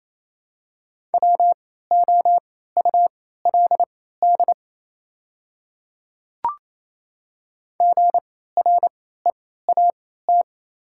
Morse code, heard as a keyed tone, about 28 wpm, spaced as for 16 wpm, 700 Hz. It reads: WOULD GREAT
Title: 3x Character Spacing - Top 300 Words - 28wpm